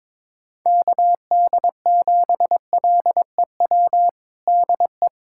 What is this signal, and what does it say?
Morse code, heard as a keyed tone, about 22 wpm, 700 Hz. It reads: KD7LEW DE
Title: USA Callsign DE Callsign 22wpm